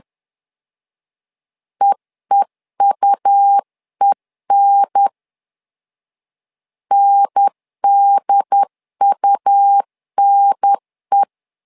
Morse code, heard as a keyed tone, about 10 wpm, 800 Hz. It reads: EEUEN NDUNE
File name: LEZIONI / 2 UHDN / 2 UHDN 10wpm 1904